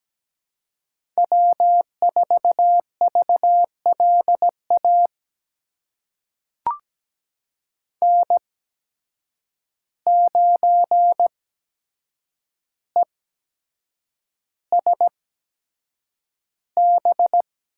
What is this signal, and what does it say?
Morse code, heard as a keyed tone, about 17 wpm, spaced as for 7 wpm, 700 Hz. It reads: W4VLA N9ESB
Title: USA Call Signs - 5 Characters - 8x Character Spacing - 17wpm